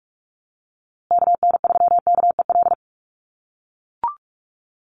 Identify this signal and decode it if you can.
Morse code, heard as a keyed tone, about 44 wpm, 700 Hz. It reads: KN3KEL
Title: USA Call Signs Repeated 3x 45wpm